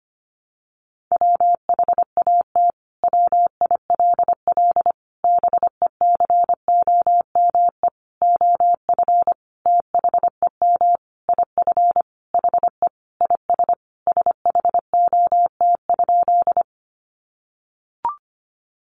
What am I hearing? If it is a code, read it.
Morse code, heard as a keyed tone, about 25 wpm, 700 Hz. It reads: WHAT WILL BECOME OF THEM IF HE IS SHOT?
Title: Sentences from Top 1100 Words 25wpm